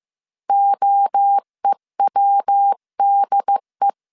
Morse code, heard as a keyed tone, about 15 wpm, 800 Hz. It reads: OEWDE
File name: CW / DET / 1 3 1500C S C 14wpm 2458